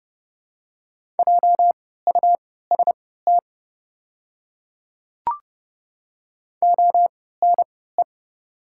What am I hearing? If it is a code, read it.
Morse code, heard as a keyed tone, about 29 wpm, spaced as for 17 wpm, 700 Hz. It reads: JUST ONE
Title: 3x Character Spacing - Top 300 Words - 30wpm